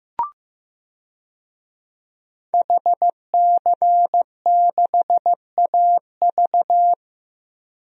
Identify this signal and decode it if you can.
Morse code, heard as a keyed tone, about 16 wpm, 700 Hz. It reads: HC6AV
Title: DXCC-100 Call Signs - 5 Characters - Repeated 2x - 15wpm